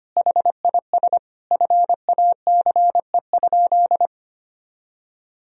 Morse code, heard as a keyed tone, about 25 wpm, 700 Hz. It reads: HIS FACE?
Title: Sentences from Top 1000 Words - Review Part 2 - 25wpm